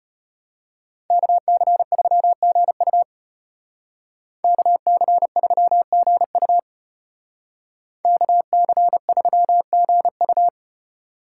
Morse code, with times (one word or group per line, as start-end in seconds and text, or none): 1.1-3.0 KC3GU
4.4-6.6 KC3GU
8.0-10.5 KC3GU